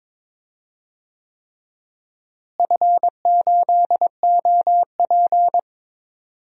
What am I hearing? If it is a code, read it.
Morse code, heard as a keyed tone, about 22 wpm, 700 Hz. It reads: F8OP
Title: DXCC-100 Call Signs - 4 Characters - Repeated 3x - 22wpm